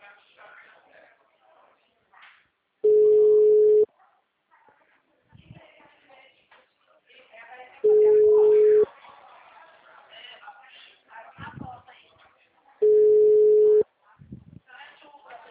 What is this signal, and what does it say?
Ringback tone (Europe, 425 Hz), with speech.